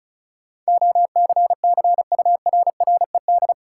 Morse code, heard as a keyed tone, about 36 wpm, 700 Hz. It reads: OCCURRED
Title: Sets of 5 Words Encore - New Word Drill - 35wpm